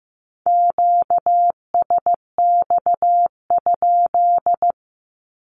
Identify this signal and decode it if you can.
Morse code, heard as a keyed tone, about 15 wpm, 700 Hz. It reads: QSX?
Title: Q-Codes 15wpm